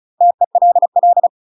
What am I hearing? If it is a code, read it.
Morse code, heard as a keyed tone, about 34 wpm, 700 Hz. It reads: TELL
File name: UK MASTER QSOs TRAINING/35WPM